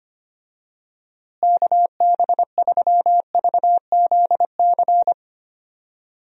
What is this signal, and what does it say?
Morse code, heard as a keyed tone, about 25 wpm, 700 Hz. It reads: KB3VZC
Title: USA Call Signs - 6 Characters - Repeated 3x - 25wpm